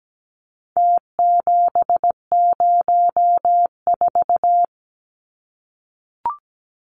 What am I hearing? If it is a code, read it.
Morse code, heard as a keyed tone, about 17 wpm, 700 Hz. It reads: T704
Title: Numbers - Contest Style - 4-Digit 17wpm